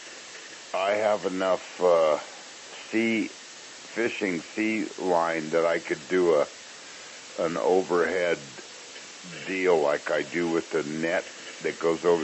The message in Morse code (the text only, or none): none